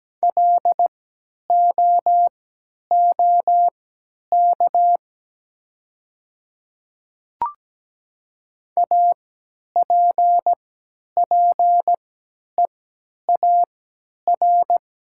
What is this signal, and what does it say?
Morse code, heard as a keyed tone, about 17 wpm, spaced as for 10 wpm, 700 Hz. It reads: LOOK APPEAR
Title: 3x Character Spacing - Top 500 Words - 17wpm